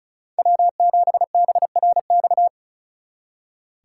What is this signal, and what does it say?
Morse code, heard as a keyed tone, about 43 wpm, 700 Hz. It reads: W7BRX
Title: Speed Racing - USA 1x3 Call Signs 35wpm